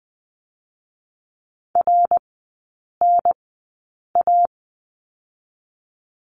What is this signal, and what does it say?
Morse code, heard as a keyed tone, about 20 wpm, spaced as for 8 wpm, 700 Hz. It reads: RNA
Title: 100 Lesson 17e - Letter u - Three Characters 20wpm